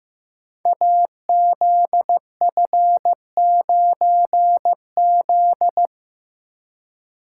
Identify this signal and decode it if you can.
Morse code, heard as a keyed tone, about 15 wpm, 700 Hz. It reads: AZF9Z
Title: Code Groups - 5 Letter:Numbers - 15wpm